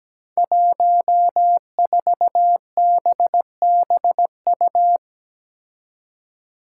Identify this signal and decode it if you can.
Morse code, heard as a keyed tone, about 17 wpm, 700 Hz. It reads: 14BBU
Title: Five Challenging Letter-Number Combinations 17wpm